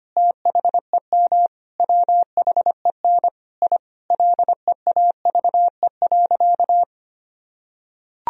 Morse code, heard as a keyed tone, about 25 wpm, 700 Hz. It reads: THEM WHEN I LEAVE.
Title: Sentences from Top 1000 Words - Review Part 2 - 25wpm